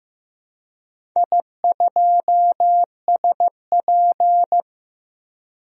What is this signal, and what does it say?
Morse code, heard as a keyed tone, about 15 wpm, 700 Hz. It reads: I2SP
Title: DXCC-100 Call Signs - 4 Characters - 15wpm